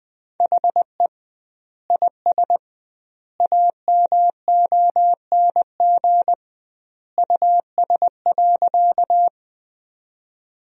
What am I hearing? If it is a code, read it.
Morse code, heard as a keyed tone, about 20 wpm, 700 Hz. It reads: HE IS AMONG US.
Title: 2x Word-Spacing - Sentences from Top 500 Words - 20wpm